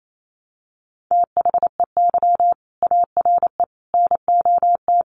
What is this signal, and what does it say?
Morse code, heard as a keyed tone, about 28 wpm, 700 Hz. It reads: THEY ARE NOT
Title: Sets of 5 Words Encore2 28wpm